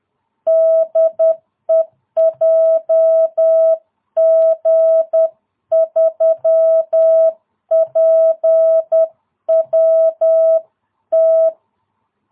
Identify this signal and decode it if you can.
Morse code, heard as a keyed tone, about 10 wpm, 660 Hz. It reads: DE JG3PWT